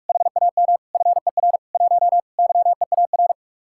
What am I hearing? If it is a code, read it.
Morse code, heard as a keyed tone, about 56 wpm, 700 Hz. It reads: HAM FER 1 YEAR